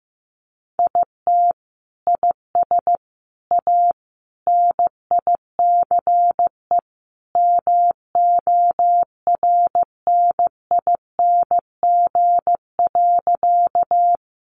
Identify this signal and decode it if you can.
Morse code, heard as a keyed tone, about 15 wpm, 700 Hz. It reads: IT IS A NICE MORNING.
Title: Sentences from Top 1100 Words 15wpm